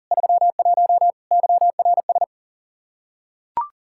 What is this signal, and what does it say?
Morse code, heard as a keyed tone, about 37 wpm, 700 Hz. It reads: 31 YRS
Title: Practice QSO - Exchange 3 - More Info - 40wpm